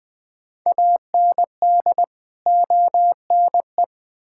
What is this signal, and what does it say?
Morse code, heard as a keyed tone, about 20 wpm, 700 Hz. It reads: AND ONE
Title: Build A Phrase - 5-Word Phrases - 20wpm